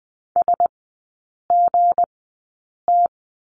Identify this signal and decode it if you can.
Morse code, heard as a keyed tone, about 20 wpm, spaced as for 8 wpm, 690 Hz. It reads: SGT
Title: 159 Lesson 26e - Letter g - Three Characters 20wpm